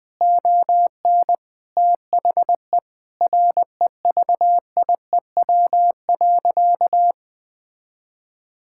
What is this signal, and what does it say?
Morse code, heard as a keyed tone, about 20 wpm, 700 Hz. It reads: ON THE REVIEW.
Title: Sentences from Top 1000 Words - Review Part 6 - 20wpm